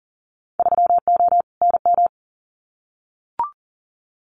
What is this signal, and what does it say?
Morse code, heard as a keyed tone, about 42 wpm, 700 Hz. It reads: 3O NM
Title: Winter Field Day Exchanges Encore 40wpm